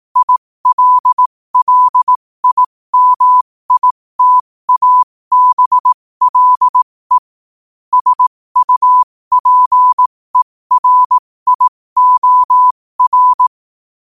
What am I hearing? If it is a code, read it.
Morse code, heard as a keyed tone, about 18 wpm, spaced as for 15 wpm, 1000 Hz. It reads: ILLIMITABLE SUPERIOR
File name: qotdfiles / 15wpm